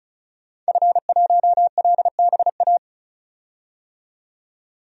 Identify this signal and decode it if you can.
Morse code, heard as a keyed tone, about 36 wpm, 700 Hz. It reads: F1LBA
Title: DXCC-100 Call Signs - 5 Characters - 35wpm